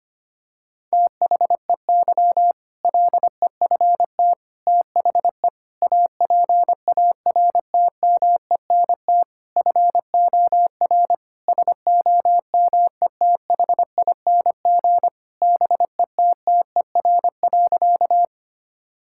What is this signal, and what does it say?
Morse code, heard as a keyed tone, about 25 wpm, 700 Hz. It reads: THEY LEFT THE APARTMENT FOR SOMETHING BETTER.
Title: Sentences from Top 1000 Words - Review Part 6 - 25wpm